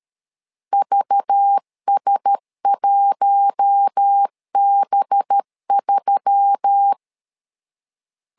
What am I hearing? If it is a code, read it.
Morse code, heard as a keyed tone, about 13 wpm, 800 Hz. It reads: VS1B3
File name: LEZIONI / 5 LBVC / 5 LBV 12wpm 1855